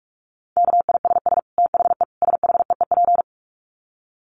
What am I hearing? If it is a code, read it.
Morse code, heard as a keyed tone, about 44 wpm, 700 Hz. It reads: KISS THE SHEEP
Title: Dit Challenge 45wpm